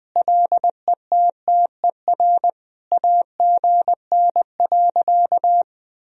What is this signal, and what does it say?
Morse code, heard as a keyed tone, about 20 wpm, 700 Hz. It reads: LETTER AGN.
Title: Sentences from Top 300 Words and Abbreviations 20wpm